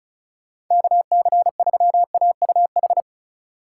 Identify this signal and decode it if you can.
Morse code, heard as a keyed tone, about 40 wpm, 700 Hz. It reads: KC3AUH